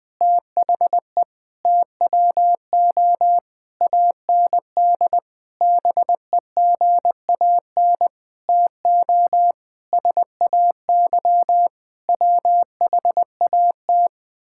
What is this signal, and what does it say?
Morse code, heard as a keyed tone, about 20 wpm, 700 Hz. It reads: THE TWO AND BEGAN TO SAY WHAT
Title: Sentences from Top 1000 Words - Review Part 5 - 20wpm